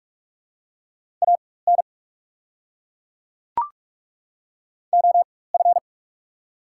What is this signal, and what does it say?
Morse code, heard as a keyed tone, about 42 wpm, spaced as for 20 wpm, 690 Hz. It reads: AN OF
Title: 4x Character Spacing - Top 300 Words - 45wpm